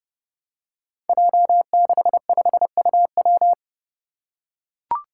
Morse code, heard as a keyed tone, about 28 wpm, 700 Hz. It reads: J65UW
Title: Five Challenging Letter-Number Combinations 30wpm